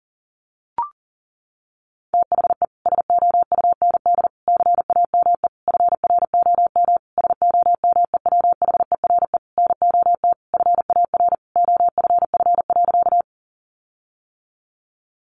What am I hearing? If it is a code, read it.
Morse code, heard as a keyed tone, about 40 wpm, 700 Hz. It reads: THE SOUND CAME FROM SOMEWHERE NOT FAR OFF.
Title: Sentences from Top 1400 Words 40wpm